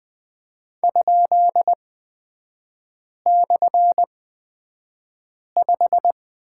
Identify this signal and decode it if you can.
Morse code, heard as a keyed tone, about 20 wpm, spaced as for 5 wpm, 700 Hz. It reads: ?/5
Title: 255 Lesson 41e - Period - Three Characters 20wpm